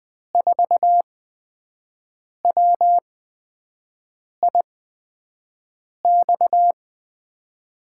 Morse code, heard as a keyed tone, about 20 wpm, spaced as for 5 wpm, 700 Hz. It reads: 4WIX
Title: USA Call Signs - 8x Character Spacing - 20wpm